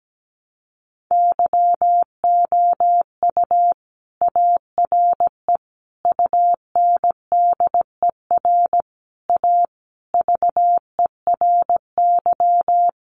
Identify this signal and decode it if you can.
Morse code, heard as a keyed tone, about 17 wpm, 700 Hz. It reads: YOU ARE UNDER A VERY
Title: Sentences from Top 1200 Words 17wpm